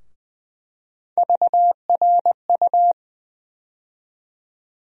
Morse code, heard as a keyed tone, about 21 wpm, 700 Hz. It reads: VRU
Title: USA Call Sign Postfixes - 3 Letters 20wpm